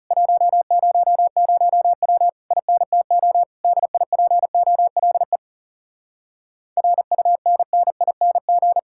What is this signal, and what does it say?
Morse code, heard as a keyed tone, about 40 wpm, 700 Hz. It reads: 100W INTO DIPOLE RUNNING